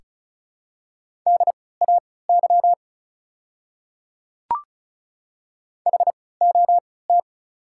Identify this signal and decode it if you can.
Morse code, heard as a keyed tone, about 36 wpm, spaced as for 20 wpm, 690 Hz. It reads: DAY HOT
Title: 3x Character Spacing - Top 100 Words - 35wpm